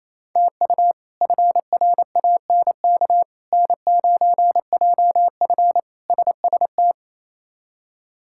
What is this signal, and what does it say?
Morse code, heard as a keyed tone, about 28 wpm, 700 Hz. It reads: TU FRANK N9JF SST
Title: K1USN SST QSO Practice - 28wpm